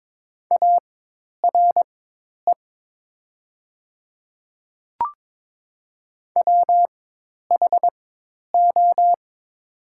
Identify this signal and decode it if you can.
Morse code, heard as a keyed tone, about 22 wpm, spaced as for 10 wpm, 700 Hz. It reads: ARE WHO
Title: 4x Character Spacing - Top 300 Words - 22wpm